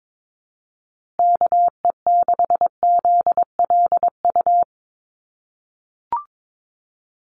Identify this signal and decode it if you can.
Morse code, heard as a keyed tone, about 22 wpm, 700 Hz. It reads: KE6ZLU
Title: USA Call Signs Repeated 2x 22wpm